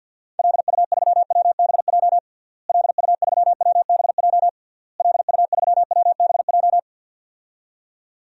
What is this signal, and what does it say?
Morse code, heard as a keyed tone, about 51 wpm, 700 Hz. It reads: LU3WBJ LU3WBJ LU3WBJ
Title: DXCC-100 Call Signs - 6 Characters - Repeated 3x - 50wpm